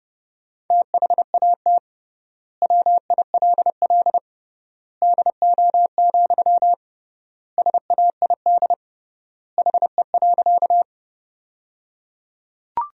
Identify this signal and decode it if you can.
Morse code, heard as a keyed tone, about 30 wpm, 700 Hz. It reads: THAT WILL DO, SAID HE.
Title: 3x Word-Spacing - Sentences from Top 100 Words - 30wpm